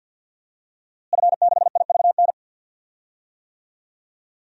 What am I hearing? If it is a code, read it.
Morse code, heard as a keyed tone, about 52 wpm, 700 Hz. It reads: F6IVN
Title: DXCC-100 Call Signs - 5 Characters - Repeated 3x - 50wpm